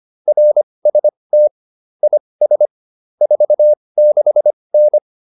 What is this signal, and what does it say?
Morse code, heard as a keyed tone, about 25 wpm, spaced as for 20 wpm, 600 Hz. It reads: RST IS 46N